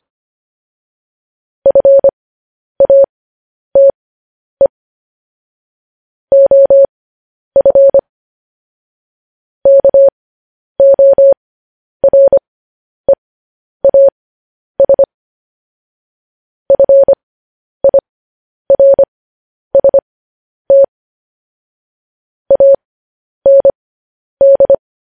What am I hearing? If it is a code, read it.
Morse code, heard as a keyed tone, about 25 wpm, spaced as for 10 wpm, 550 Hz. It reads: FATE OF KOREAS FIRST AND